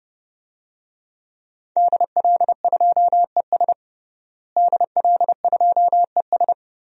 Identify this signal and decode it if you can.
Morse code, heard as a keyed tone, about 30 wpm, 700 Hz. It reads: DL2ES DL2ES
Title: DXCC-100 Call Signs - 5 Characters - Repeated 2x - 30wpm